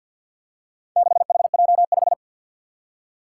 Morse code, heard as a keyed tone, about 52 wpm, 700 Hz. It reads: 6HJ5
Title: Four Challenging Letter-Number Combinations 50wpm